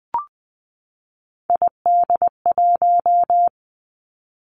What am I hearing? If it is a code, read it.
Morse code, heard as a keyed tone, about 20 wpm, 700 Hz. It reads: ID1
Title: Three Challenging Letter-Number Combinations 20wpm